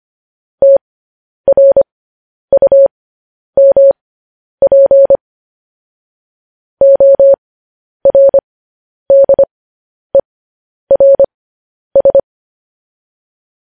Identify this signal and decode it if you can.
Morse code, heard as a keyed tone, about 25 wpm, spaced as for 10 wpm, 550 Hz. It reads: TRUMP ORDERS
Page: news-headlines-in-morse-code-at-10-wpm